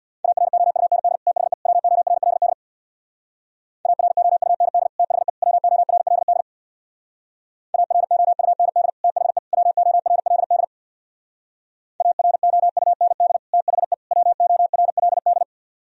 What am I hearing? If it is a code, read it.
Morse code, heard as a keyed tone, about 60 wpm, 700 Hz. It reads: AROUND THE WORLD AROUND THE WORLD AROUND THE WORLD AROUND THE WORLD
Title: Speed-Racing - Sets of 3 Words 50wpm